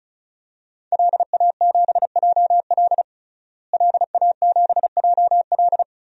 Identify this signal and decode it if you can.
Morse code, heard as a keyed tone, about 36 wpm, 700 Hz. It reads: LA7JL LA7JL
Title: DXCC-100 Call Signs - 5 Characters - Repeated 3x - 35wpm